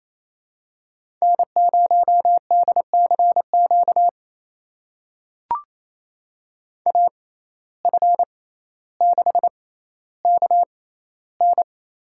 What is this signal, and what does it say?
Morse code, heard as a keyed tone, about 28 wpm, spaced as for 14 wpm, 700 Hz. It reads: N0DCQ AF6KN
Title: USA Call Signs - 6x Character Spacing - 28wpm